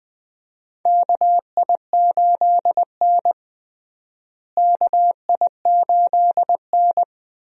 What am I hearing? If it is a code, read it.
Morse code, heard as a keyed tone, about 20 wpm, 700 Hz. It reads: KI8N KI8N
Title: USA Call Signs - 4 Characters - Repeated 2x - 20wpm